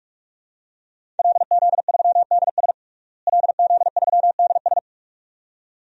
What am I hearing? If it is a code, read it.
Morse code, heard as a keyed tone, about 46 wpm, 700 Hz. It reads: LZ3DS LZ3DS
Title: DXCC-100 Call Signs - 5 Characters - Repeated 2x - 45wpm